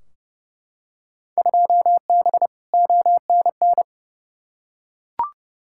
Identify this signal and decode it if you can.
Morse code, heard as a keyed tone, about 30 wpm, 700 Hz. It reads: 2B ONN
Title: Field Day Exchanges - 30wpm